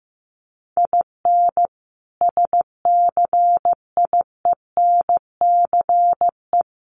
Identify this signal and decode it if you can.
Morse code, heard as a keyed tone, about 15 wpm, 700 Hz. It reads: IN SCIENCE
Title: Sentences from Top 1400 Words 15wpm